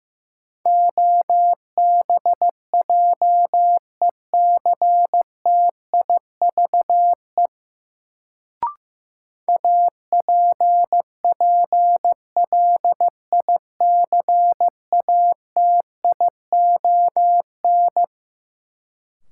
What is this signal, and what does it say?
Morse code, heard as a keyed tone, about 15 wpm, 700 Hz. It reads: OBJECTIVE APPLICATION
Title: Sentences from Top 1200 Words - New Word Drill - 15wpm